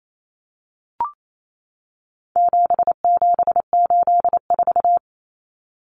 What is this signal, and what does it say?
Morse code, heard as a keyed tone, about 28 wpm, 700 Hz. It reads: 7784
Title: Common Cut Numbers - 4-Digit 28wpm